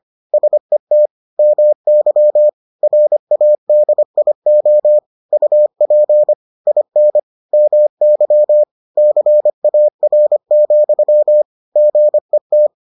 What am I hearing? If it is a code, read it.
Morse code, heard as a keyed tone, about 25 wpm, 600 Hz. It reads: SET MY RADIO UP IN MY CAR, GET